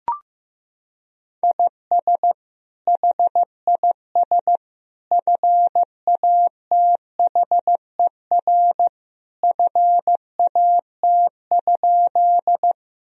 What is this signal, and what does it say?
Morse code, heard as a keyed tone, about 16 wpm, 700 Hz. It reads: IS HIS FATHER FAT?
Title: Sentences from Top 1000 Words - Review Part 4 - 15wpm